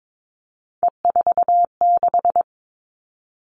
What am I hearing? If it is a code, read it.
Morse code, heard as a keyed tone, about 22 wpm, 700 Hz. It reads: E46
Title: All Cut Numbers - 3-Digit 22wpm